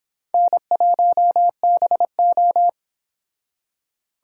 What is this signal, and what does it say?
Morse code, heard as a keyed tone, about 25 wpm, 700 Hz. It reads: N1BO